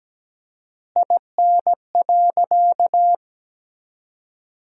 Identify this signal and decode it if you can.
Morse code, heard as a keyed tone, about 17 wpm, 700 Hz. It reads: IN.
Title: 4x Word-Spacing - Sentences from Top 400 Words - 17wpm